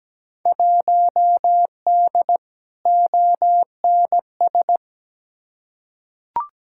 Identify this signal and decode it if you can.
Morse code, heard as a keyed tone, about 17 wpm, 700 Hz. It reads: 1D ONS
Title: Field Day Exchanges - 17wpm